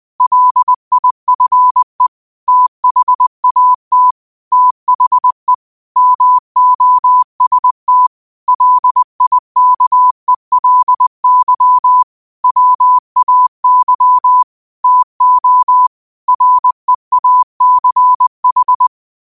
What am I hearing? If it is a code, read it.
Morse code, heard as a keyed tone, about 20 wpm, 1000 Hz. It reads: LIFE THAT THE MOST LIKELY WAY TO REACH
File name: qotdfiles / 20wpm